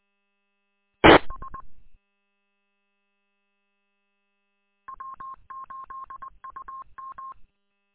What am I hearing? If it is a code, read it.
Morse code, heard as a keyed tone, about 20 wpm, 1070 Hz. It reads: TS W8UM
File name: w8um- -combined